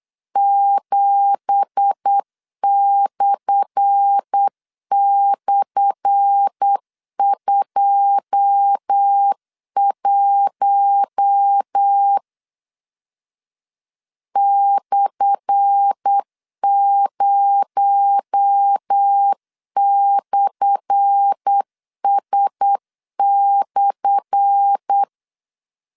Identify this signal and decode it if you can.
Morse code, heard as a keyed tone, about 8 wpm, 800 Hz. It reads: 7//21 /0/S/
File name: LEZIONI / 10 SIGNS / 10A SLASH 8wpm 3452